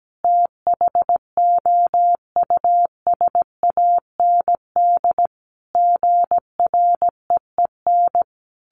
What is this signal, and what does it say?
Morse code, heard as a keyed tone, about 17 wpm, 700 Hz. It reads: THOUSAND GREEN
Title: A Sound Of Thunder 17wpm